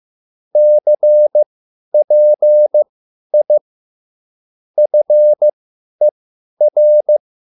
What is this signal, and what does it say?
Morse code, heard as a keyed tone, about 15 wpm, spaced as for 10 wpm, 600 Hz. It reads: CPI FER